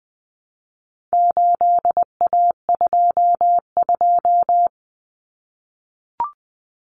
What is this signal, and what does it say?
Morse code, heard as a keyed tone, about 20 wpm, 700 Hz. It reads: 8A22
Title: Common Cut Numbers - 4-Digit 20wpm